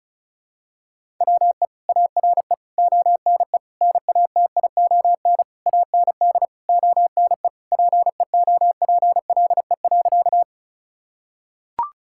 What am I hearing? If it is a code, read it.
Morse code, heard as a keyed tone, about 34 wpm, 700 Hz. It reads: WE ARE ONE NATION AND ONE PEOPLE.
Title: Sentences from Top 1000 Words - Review Part 6 - 35wpm